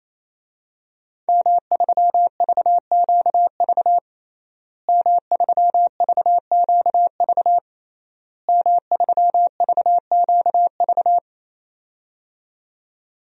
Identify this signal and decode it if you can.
Morse code, heard as a keyed tone, about 28 wpm, 700 Hz. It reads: M3VQV M3VQV M3VQV
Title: DXCC-100 Call Signs - 5 Characters - Repeated 3x - 28wpm